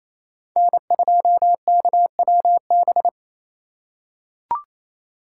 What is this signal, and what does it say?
Morse code, heard as a keyed tone, about 28 wpm, 700 Hz. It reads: N2KWB
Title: USA Call Signs - 4x Character Spacing - 28wpm